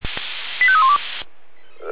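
Five-tone selective call (ZVEI), heard as quick runs of five tones, tones of about 70 ms. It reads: 85312
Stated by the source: Five-tone ZVEI selective call 85312 received by SDR.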